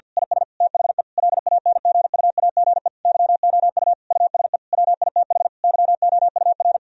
Morse code, heard as a keyed tone, about 50 wpm, 700 Hz. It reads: IS THE LANGUAGE YOU USE WITH YOUR
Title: Sentences from Top 1000 Words - Review Part 3 - 50wpm